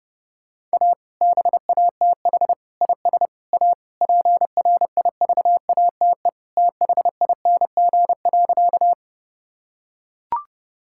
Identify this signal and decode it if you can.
Morse code, heard as a keyed tone, about 30 wpm, 700 Hz. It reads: A BATH IS A PRIVATE THING.
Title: Sentences from Top 1600 Words 30wpm